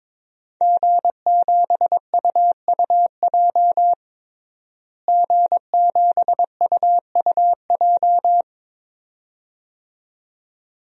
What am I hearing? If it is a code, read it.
Morse code, heard as a keyed tone, about 22 wpm, 700 Hz. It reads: G7UUJ G7UUJ
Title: DXCC-100 Call Signs - 5 Characters - Repeated 2x - 22wpm